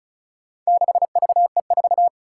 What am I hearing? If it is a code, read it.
Morse code, heard as a keyed tone, about 36 wpm, 700 Hz. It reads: 6VE4
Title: Four Challenging Letter-Number Combinations 35wpm